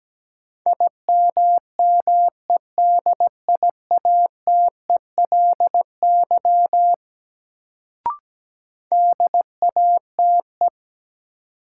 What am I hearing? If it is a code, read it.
Morse code, heard as a keyed tone, about 17 wpm, 700 Hz. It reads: IMMEDIATELY DATE
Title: Sentences from Top 1100 Words - New Word Drill Alt - 17wpm